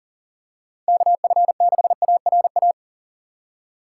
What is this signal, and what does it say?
Morse code, heard as a keyed tone, about 40 wpm, 700 Hz. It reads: KF6ARA
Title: USA Call Signs - 6 Characters - 2x Character Spacing - 40wpm